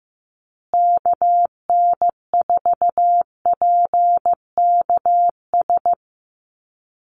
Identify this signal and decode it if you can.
Morse code, heard as a keyed tone, about 15 wpm, 700 Hz. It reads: KN4PKS
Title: USA Call Signs Repeated 2x 15wpm